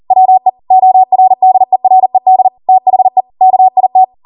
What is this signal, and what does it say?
Morse code, heard as a keyed tone, about 40 wpm, 750 Hz. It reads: WE ORDERED THE KIT
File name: Morse / 40WPM